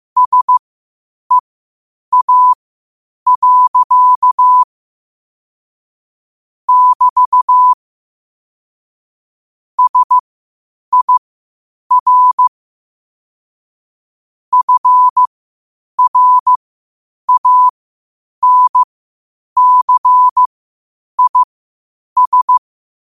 Morse code, heard as a keyed tone, about 15 wpm, spaced as for 8 wpm, 1000 Hz. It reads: SEA. <BT> SIR FRANCIS
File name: qotdfiles / 8wpm